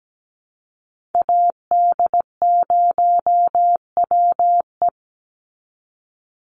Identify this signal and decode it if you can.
Morse code, heard as a keyed tone, about 17 wpm, 700 Hz. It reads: AD0WE
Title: Call Signs - 17wpm